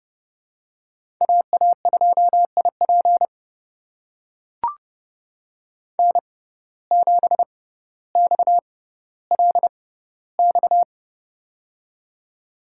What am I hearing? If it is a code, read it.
Morse code, heard as a keyed tone, about 32 wpm, spaced as for 16 wpm, 700 Hz. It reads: AA2IP N7XLX
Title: USA Call Signs - 5 Characters - 6x Character Spacing - 30wpm